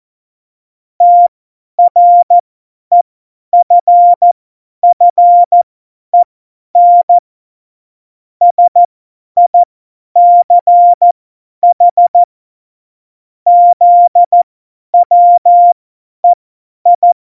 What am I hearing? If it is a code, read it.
Morse code, heard as a keyed tone, about 14 wpm, spaced as for 10 wpm, 700 Hz. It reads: TREFFEN SICH ZWEI
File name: Morsekurs / Witze / Witz 21. August 2018, 50 F70